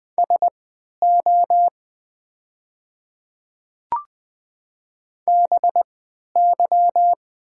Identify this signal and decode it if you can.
Morse code, heard as a keyed tone, about 20 wpm, spaced as for 11 wpm, 690 Hz. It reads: SO BY
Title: 3x Character Spacing - Top 200 Words - 20wpm